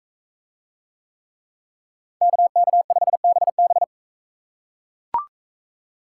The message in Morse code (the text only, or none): KK5BB